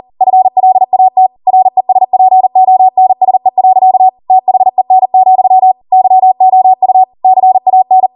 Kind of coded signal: Morse code